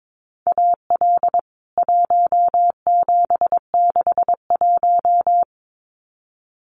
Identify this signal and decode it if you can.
Morse code, heard as a keyed tone, about 22 wpm, 700 Hz. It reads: AL 1761
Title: CWT Exchange 22wpm